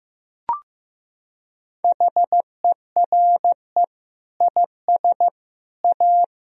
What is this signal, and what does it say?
Morse code, heard as a keyed tone, about 15 wpm, 700 Hz. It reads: HERE IS A
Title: Sentences from Top 1000 Words - Review Part 1 - 15wpm